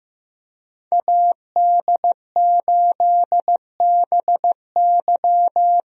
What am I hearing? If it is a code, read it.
Morse code, heard as a keyed tone, about 15 wpm, 700 Hz. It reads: AD8BY
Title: USA Call Signs - 5 Characters - Repeated 2x - 15wpm